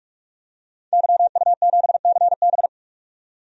Morse code, heard as a keyed tone, about 43 wpm, 700 Hz. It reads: YU7CB
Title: DXCC-100 Call Signs - 5 Characters - 45wpm